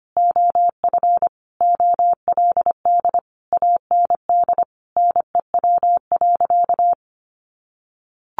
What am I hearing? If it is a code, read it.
Morse code, heard as a keyed tone, about 25 wpm, 700 Hz. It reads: OF OLD AND NEW.
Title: Sentences from Top 1500 Words 25wpm